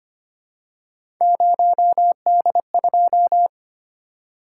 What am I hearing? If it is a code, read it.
Morse code, heard as a keyed tone, about 25 wpm, 700 Hz. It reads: 0D2
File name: Three Letter-Number Combinations - 25wpm